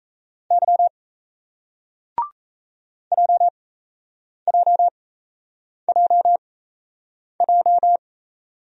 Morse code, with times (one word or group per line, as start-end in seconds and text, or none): 0.5-0.9 Y
3.1-8.0 JJJJ